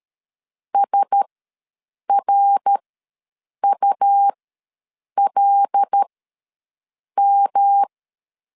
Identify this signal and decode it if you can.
Morse code, heard as a keyed tone, about 13 wpm, spaced as for 7 wpm, 800 Hz. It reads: SRULM
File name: LEZIONI / 5 LBVC / 5 LBVC ws 12wpm 3159